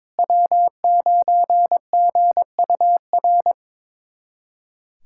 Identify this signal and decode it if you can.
Morse code, heard as a keyed tone, about 21 wpm, 700 Hz. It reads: W9GUR
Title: USA Callsign DE Callsign - 5 Characters - 22wpm